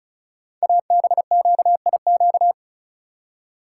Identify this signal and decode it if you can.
Morse code, heard as a keyed tone, about 36 wpm, 690 Hz. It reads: ABQIQ
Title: Code Groups - 5 Letters - 35wpm